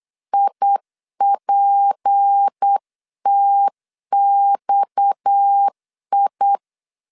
Morse code, heard as a keyed tone, about 8 wpm, 800 Hz. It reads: IPTXI
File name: 8wpm